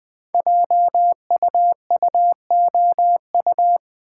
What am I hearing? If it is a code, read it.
Morse code, heard as a keyed tone, about 20 wpm, 700 Hz. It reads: JUUOU
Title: Code Groups - 5 Letters - 20wpm